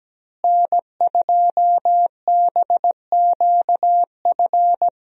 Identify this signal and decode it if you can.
Morse code, heard as a keyed tone, about 17 wpm, 700 Hz. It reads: N2BQF
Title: USA Call Signs - 5 Characters - Repeated 3x - 17wpm